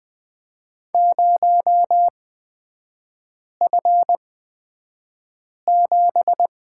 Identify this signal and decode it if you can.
Morse code, heard as a keyed tone, about 20 wpm, spaced as for 5 wpm, 700 Hz. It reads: 0F7
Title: 243 Lesson 39e - Letter z - Three Characters 20wpm